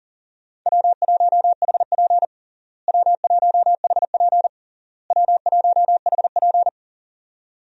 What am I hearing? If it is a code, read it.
Morse code, heard as a keyed tone, about 40 wpm, 700 Hz. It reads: W1HP W1HP W1HP
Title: USA Call Signs - 4 Characters - Repeated 3x - 40wpm